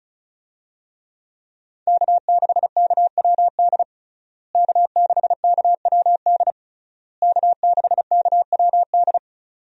Morse code, heard as a keyed tone, about 34 wpm, 700 Hz. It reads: K6KWD K6KWD K6KWD
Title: USA Call Signs - 5 Characters - Repeated 3x - 35wpm